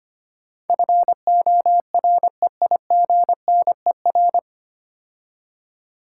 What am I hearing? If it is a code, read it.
Morse code, heard as a keyed tone, about 25 wpm, 700 Hz. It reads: FOREIGNER
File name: practice / 25wpm / 9 Letter Words - First 3 Letters Given - 25wpm